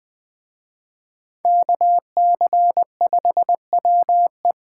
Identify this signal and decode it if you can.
Morse code, heard as a keyed tone, about 23 wpm, 700 Hz. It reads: KC5WE
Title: Speed Racing - USA 2x2 Call Signs 20wpm